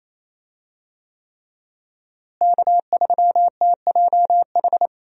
Morse code, heard as a keyed tone, about 28 wpm, 700 Hz. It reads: K3TJH
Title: USA Call Signs - 5 Characters - Repeated 3x - 28wpm